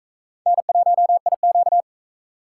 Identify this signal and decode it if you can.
Morse code, heard as a keyed tone, about 43 wpm, 700 Hz. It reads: N1IQ